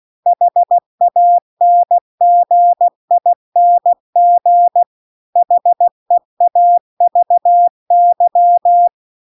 Morse code, heard as a keyed tone, about 16 wpm, 700 Hz. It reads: HANGING HEAVY
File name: CW-The-Big-Twoheaded-River-2-16WPM